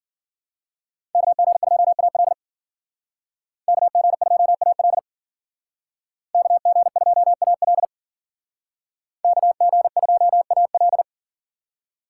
Morse code, heard as a keyed tone, about 52 wpm, 700 Hz. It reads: KG2AL KG2AL KG2AL KG2AL